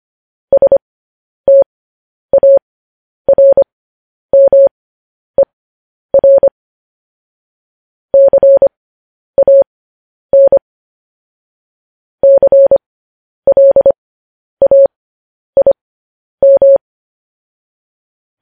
Morse code, heard as a keyed tone, about 25 wpm, spaced as for 10 wpm, 550 Hz. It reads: STARMER CAN CLAIM
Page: News Headlines in Morse Code at 10 WPM